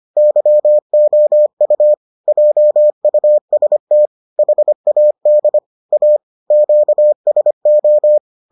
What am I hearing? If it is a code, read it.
Morse code, heard as a keyed tone, about 25 wpm, 600 Hz. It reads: YOU JUST HAD A QSO